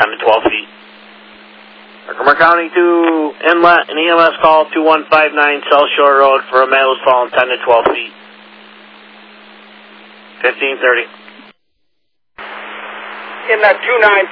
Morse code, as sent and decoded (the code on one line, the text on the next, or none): none